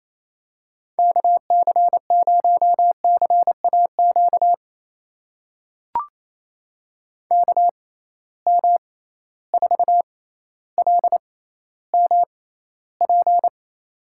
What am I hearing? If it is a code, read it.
Morse code, heard as a keyed tone, about 28 wpm, spaced as for 14 wpm, 700 Hz. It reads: KC0CAQ KM4LMP
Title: USA Call Signs - 6 Characters - 6x Character Spacing - 28wpm